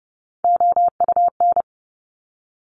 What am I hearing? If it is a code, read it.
Morse code, heard as a keyed tone, about 30 wpm, 700 Hz. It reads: OUN
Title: Top Three Letter Combinations in Words 30wpm